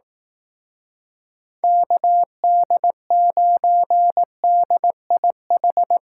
Morse code, heard as a keyed tone, about 18 wpm, 700 Hz. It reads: KD9DIH
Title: Speed Racing - USA 2x3 Call Signs 15wpm